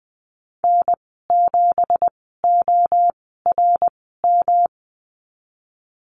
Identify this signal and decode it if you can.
Morse code, heard as a keyed tone, about 20 wpm, spaced as for 15 wpm, 700 Hz. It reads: N7ORM
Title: 186 Lesson 30g - Letter b - Call Signs 20wpm